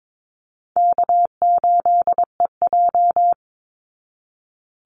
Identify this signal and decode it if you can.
Morse code, heard as a keyed tone, about 22 wpm, 700 Hz. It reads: K8EJ
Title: Easy Call Signs Repeated 3x - 22wpm